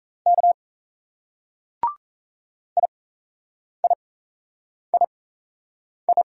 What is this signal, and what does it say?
Morse code, heard as a keyed tone, about 33 wpm, spaced as for 8 wpm, 700 Hz. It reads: K IIII
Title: Speed-Racing Single Letters 28wpm